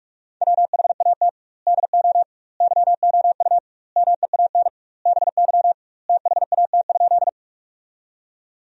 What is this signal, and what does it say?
Morse code, heard as a keyed tone, about 46 wpm, 700 Hz. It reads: WHAT DO YOU MEAN BY THAT?
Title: 2x Word-Spacing - Sentences from Top 200 Words - 45wpm